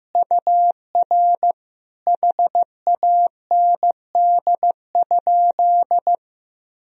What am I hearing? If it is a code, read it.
Morse code, heard as a keyed tone, about 15 wpm, 700 Hz. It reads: UR HAND?
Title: Sentences from Top 200 Words and Abbreviations 15wpm